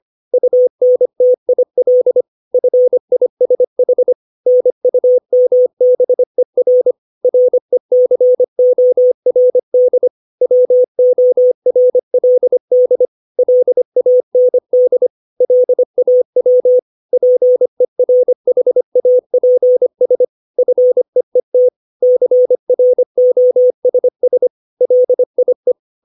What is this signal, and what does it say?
Morse code, heard as a keyed tone, about 25 wpm, 500 Hz. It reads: UNTIL FISH NUMBER RECORD WORLD LAND LAW PERHAPS FEET CROSS LIE
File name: 632 words / 632 Words 25WPM